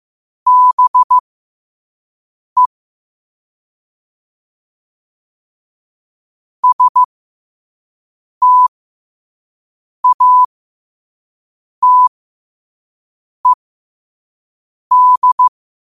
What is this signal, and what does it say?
Morse code, heard as a keyed tone, about 15 wpm, spaced as for 5 wpm, 1000 Hz. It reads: BE STATED
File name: 5wpm